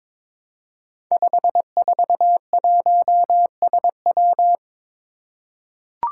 Morse code, heard as a keyed tone, about 22 wpm, 700 Hz. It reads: 541SW
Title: Five Challenging Letter-Number Combinations 22wpm